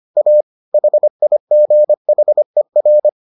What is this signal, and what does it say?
Morse code, heard as a keyed tone, about 25 wpm, 600 Hz. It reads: A HIGHER